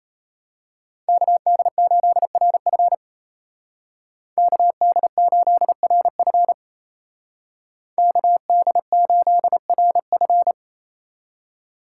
Morse code, heard as a keyed tone, about 34 wpm, 700 Hz. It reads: KD8RF KD8RF KD8RF
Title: Speed Racing - USA 2x2 Call Signs 28wpm